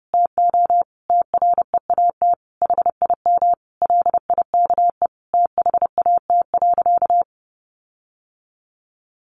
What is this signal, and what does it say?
Morse code, heard as a keyed tone, about 30 wpm, 700 Hz. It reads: TO TREAT HIM LIKE THAT.
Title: Sentences from Top 1300 Words 30wpm